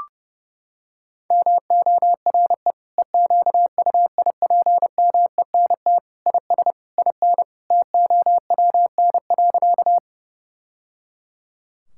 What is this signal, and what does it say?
Morse code, heard as a keyed tone, about 30 wpm, 700 Hz. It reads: MORE EQUIPMENT IS IN TOWN.
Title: Sentences from Top 1000 Words - Review Part 5 - 30wpm